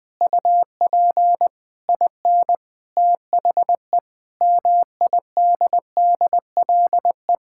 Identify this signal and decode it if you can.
Morse code, heard as a keyed tone, about 20 wpm, 700 Hz. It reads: UP IN THE MIDDLE
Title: Sentences with 8-Word Phrases - 20wpm